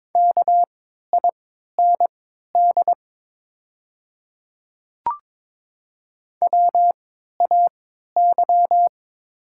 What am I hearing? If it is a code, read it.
Morse code, heard as a keyed tone, about 22 wpm, spaced as for 13 wpm, 700 Hz. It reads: KIND WAY
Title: 3x Character Spacing - Top 200 Words - 22wpm